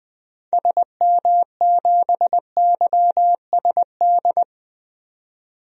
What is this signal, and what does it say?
Morse code, heard as a keyed tone, about 20 wpm, 700 Hz. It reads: SM7YSD
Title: DXCC-100 Call Signs - 6 Characters - 20wpm